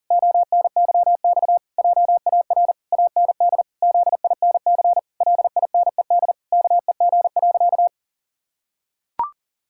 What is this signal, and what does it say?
Morse code, heard as a keyed tone, about 40 wpm, 700 Hz. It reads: ONYX JAR AND ZINC LINED KEG.
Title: Pangram Challenge v2 - 40wpm